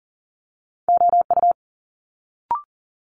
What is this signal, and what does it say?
Morse code, heard as a keyed tone, about 40 wpm, 700 Hz. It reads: OU
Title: Multi-letter Phonemes Quick Fire 40wpm